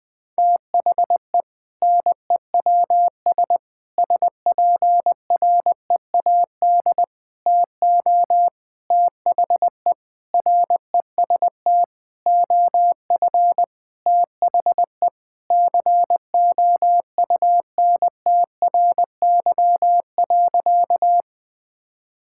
Morse code, heard as a keyed tone, about 20 wpm, 700 Hz. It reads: THE NEWS SPREAD TO THE REST OF THE COUNTRY.
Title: Sentences with 5-Word Phrases - 20wpm